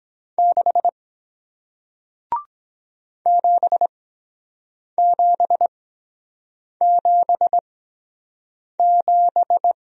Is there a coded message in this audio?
Yes: Morse code from 0.4 to 9.7 s, heard as a keyed tone, about 23 wpm, spaced as for 7 wpm, 700 Hz. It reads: 6 7777